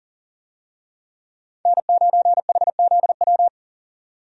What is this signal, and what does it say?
Morse code, heard as a keyed tone, about 40 wpm, 700 Hz. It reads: N9HZW
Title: USA Call Signs - 5 Characters - 40wpm